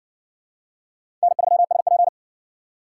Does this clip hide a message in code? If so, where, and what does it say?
Yes: Morse code from 1.2 to 2.1 s, heard as a keyed tone, about 57 wpm, 700 Hz. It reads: N3SP